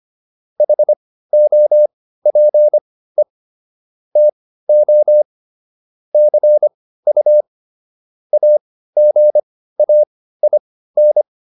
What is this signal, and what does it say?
Morse code, heard as a keyed tone, about 25 wpm, spaced as for 15 wpm, 600 Hz. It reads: HOPE TO CU AGAIN